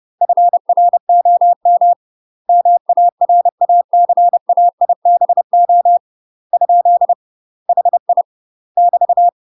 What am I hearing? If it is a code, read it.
Morse code, heard as a keyed tone, about 30 wpm, 700 Hz. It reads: FROM MARACAIBO ? HI <BT>
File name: UK MASTER QSOs TRAINING / 30WPM / UK MASTER QSOs 0022